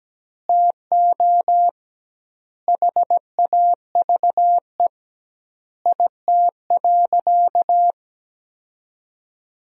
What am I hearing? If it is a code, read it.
Morse code, heard as a keyed tone, about 17 wpm, 700 Hz. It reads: TO HAVE IT.
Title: 2x Word-Spacing - Sentences from Top 100 Words - 17wpm